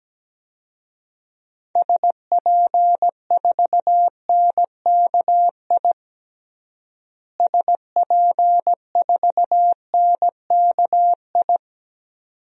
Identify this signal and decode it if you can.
Morse code, heard as a keyed tone, about 17 wpm, 700 Hz. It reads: SP4NKI SP4NKI
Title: DXCC-100 Call Signs - 6 Characters - Repeated 2x - 17wpm